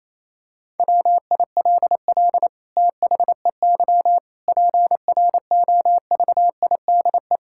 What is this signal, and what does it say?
Morse code, heard as a keyed tone, about 28 wpm, 700 Hz. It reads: WILL THEY PROVIDE
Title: Sentences from Top 1000 Words - Review Part 5 - 28wpm